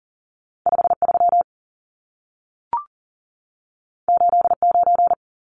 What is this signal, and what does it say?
Morse code, heard as a keyed tone, about 40 wpm, 700 Hz. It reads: 53 89
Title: Numbers - 2-Digit 40wpm